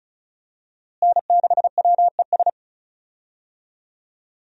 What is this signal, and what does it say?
Morse code, heard as a keyed tone, about 34 wpm, 700 Hz. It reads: N6WES
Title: USA Call Signs - 5 Characters - Repeated 3x - 35wpm